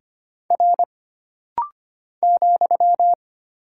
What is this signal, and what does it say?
Morse code, heard as a keyed tone, about 25 wpm, 700 Hz. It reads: R ,
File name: Single Character (Letter:Number:Punctuation:Prosign) - Rapid-Fire - 25wpm